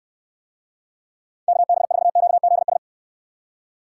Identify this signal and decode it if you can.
Morse code, heard as a keyed tone, about 65 wpm, 700 Hz. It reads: KB4QZS